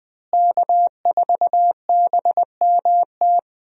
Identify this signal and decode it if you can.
Morse code, heard as a keyed tone, about 20 wpm, 700 Hz. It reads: K4BMT
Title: USA Call Signs - 5 Characters - Repeated 3x - 20wpm